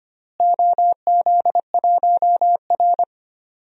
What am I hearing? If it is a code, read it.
Morse code, heard as a keyed tone, about 25 wpm, 700 Hz. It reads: OZ1R
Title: DXCC-100 Call Signs - 4 Characters - Repeated 3x - 25wpm